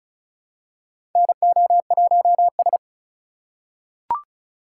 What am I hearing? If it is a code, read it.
Morse code, heard as a keyed tone, about 36 wpm, 700 Hz. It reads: NO1S
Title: USA Call Signs - 4 Characters - 3x Character Spacing - 35wpm